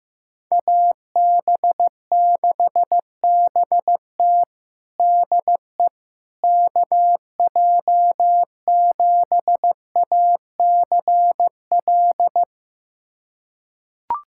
Morse code, heard as a keyed tone, about 15 wpm, 700 Hz. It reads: AB6BT DE KJ7ACL
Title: USA Callsign DE Callsign 15wpm